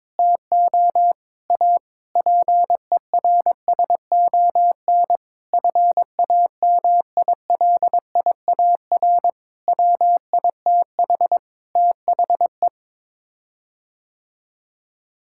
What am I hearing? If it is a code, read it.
Morse code, heard as a keyed tone, about 22 wpm, 700 Hz. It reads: TO A PERSON FAMILIAR WITH THE
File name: practice / 22wpm / Sets of 7 Words - 22wpm